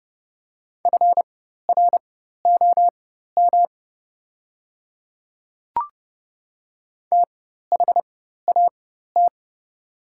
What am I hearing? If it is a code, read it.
Morse code, heard as a keyed tone, about 30 wpm, spaced as for 14 wpm, 700 Hz. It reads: FROM THAT
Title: 4x Character Spacing - Top 100 Words - 30wpm